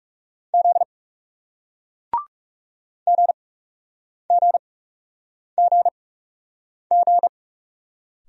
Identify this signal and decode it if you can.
Morse code, heard as a keyed tone, about 40 wpm, spaced as for 8 wpm, 700 Hz. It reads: Z GGGG